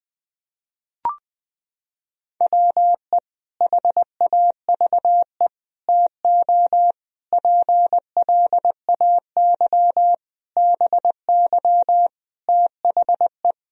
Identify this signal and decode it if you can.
Morse code, heard as a keyed tone, about 20 wpm, 700 Hz. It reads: WE HAVE TO PLAY BY THE
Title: Sentences from Top 1000 Words - Review Part 3 - 20wpm